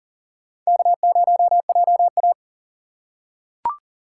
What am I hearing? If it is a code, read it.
Morse code, heard as a keyed tone, about 40 wpm, 700 Hz. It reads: K0JA
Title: USA Call Signs - 2x Character Spacing - 40wpm